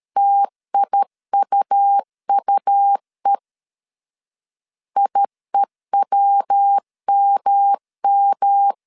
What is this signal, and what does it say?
Morse code, heard as a keyed tone, about 12 wpm, 800 Hz. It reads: TIUUE IEWMM
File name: CW / DET / 1 3 1500C S B 12wpm 2828